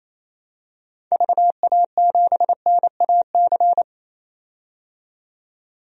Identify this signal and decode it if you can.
Morse code, heard as a keyed tone, about 28 wpm, 700 Hz. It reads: VA7NAC
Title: DXCC-100 Call Signs - 6 Characters - 28wpm